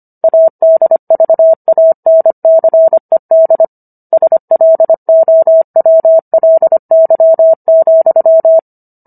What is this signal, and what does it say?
Morse code, heard as a keyed tone, about 25 wpm, 650 Hz. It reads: ADVANCED SLOWLY,